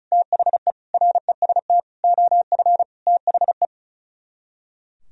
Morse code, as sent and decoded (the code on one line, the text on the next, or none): - .... / .-. ... - / --- ..-. / - .... .
THE REST OF THE